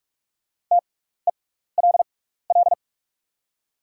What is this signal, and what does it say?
Morse code, heard as a keyed tone, about 47 wpm, spaced as for 16 wpm, 690 Hz. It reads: TELL
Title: 6x Character Spacing - Top 200 Words - 45wpm